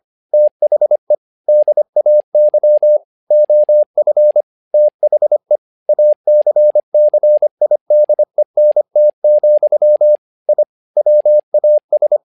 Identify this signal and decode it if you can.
Morse code, heard as a keyed tone, about 25 wpm, 600 Hz. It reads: THE DAY OF THE ACCIDENT, I WAS
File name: learning center / tower trouble / tower trouble 25wpm